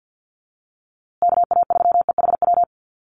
Morse code, heard as a keyed tone, about 50 wpm, 700 Hz. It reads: KA3EHW
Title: USA Call Signs Repeated 3x 50wpm